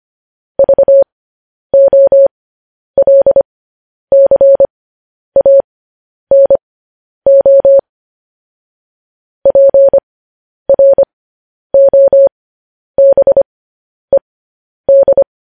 Morse code, heard as a keyed tone, about 25 wpm, spaced as for 10 wpm, 550 Hz. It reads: VOLCANO PROBED
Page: News Headlines in Morse Code at 10 WPM